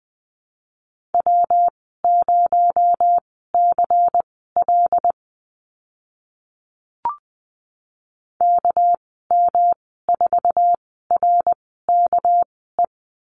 Morse code, heard as a keyed tone, about 20 wpm, spaced as for 14 wpm, 700 Hz. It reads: W0CL KM4RKE
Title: 238 Lesson 38g - Letter q - Call Signs 20wpm